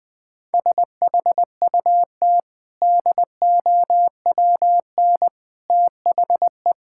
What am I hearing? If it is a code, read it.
Morse code, heard as a keyed tone, about 20 wpm, 700 Hz. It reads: SHUT DOWN THE
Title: Easy Sentences with Hints - 20wpm